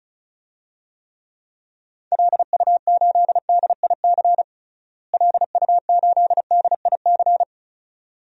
Morse code, heard as a keyed tone, about 36 wpm, 700 Hz. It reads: LU8DIC LU8DIC
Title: DXCC-100 Call Signs - 6 Characters - Repeated 2x - 35wpm